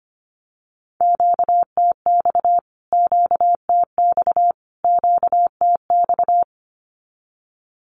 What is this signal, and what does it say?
Morse code, heard as a keyed tone, about 25 wpm, 700 Hz. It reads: QTX QTX QTX
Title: Q-Codes Repeated 3x 25wpm